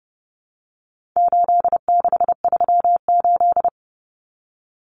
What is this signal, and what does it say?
Morse code, heard as a keyed tone, about 30 wpm, 700 Hz. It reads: 8638
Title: Common Cut Numbers - 4-Digit 30wpm